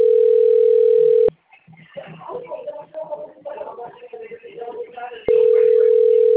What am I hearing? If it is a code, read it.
Ringback tone (North America), 440 + 480 Hz, with speech.
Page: Atibaia - SP